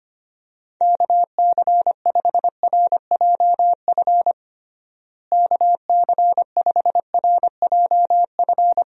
Morse code, heard as a keyed tone, about 25 wpm, 700 Hz. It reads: KC5RJF KC5RJF
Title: USA Call Signs - 6 Characters - Repeated 2x - 25wpm